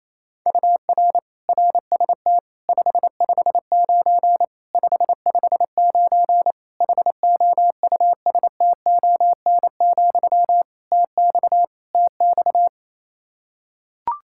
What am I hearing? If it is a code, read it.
Morse code, heard as a keyed tone, about 28 wpm, 700 Hz. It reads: UR RST 559 559 HOUSTON, TX TX
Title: K1USN SST QSO Practice 28wpm